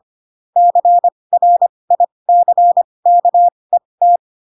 Morse code, heard as a keyed tone, about 25 wpm, spaced as for 20 wpm, 700 Hz. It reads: CRICKET
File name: Pinocchio / Pinocchio ch4 15 25 ik1hge